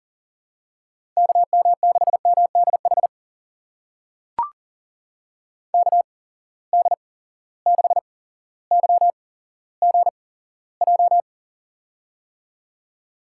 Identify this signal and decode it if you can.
Morse code, heard as a keyed tone, about 39 wpm, spaced as for 17 wpm, 700 Hz. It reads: KM6MDH KD6YGJ
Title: USA Call Signs - 6 Characters - 8x Character Spacing - 40wpm